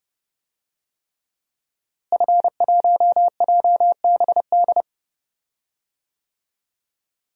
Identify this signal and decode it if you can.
Morse code, heard as a keyed tone, about 30 wpm, 700 Hz. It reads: F1JBD
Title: DXCC-100 Call Signs - 5 Characters - Repeated 3x - 30wpm